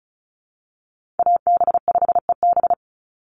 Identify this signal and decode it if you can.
Morse code, heard as a keyed tone, about 34 wpm, 700 Hz. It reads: AB5EB
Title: Call Signs Repeated 2x - 35wpm